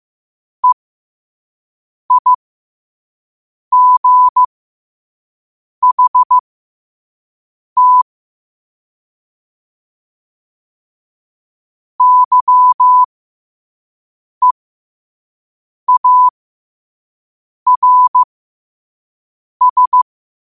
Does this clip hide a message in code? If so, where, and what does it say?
Yes: Morse code from 0.6 to 20.0 s, heard as a keyed tone, about 15 wpm, spaced as for 5 wpm, 1000 Hz. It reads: EIGHT YEARS